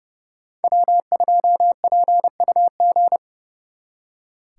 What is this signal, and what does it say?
Morse code, heard as a keyed tone, about 36 wpm, 700 Hz. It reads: W2PUG
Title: Speed Racing - USA Call Signs 30wpm